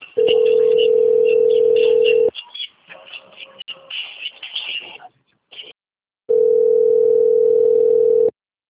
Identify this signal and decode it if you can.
Ringback tone (North America), 440 + 480 Hz.
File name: out- - - - - .3078